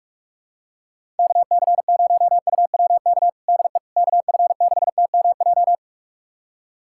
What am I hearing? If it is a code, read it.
Morse code, heard as a keyed tone, about 46 wpm, 700 Hz. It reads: KC0UWK DE KF6TMJ
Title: USA Callsign DE Callsign - 6 Characters - 45wpm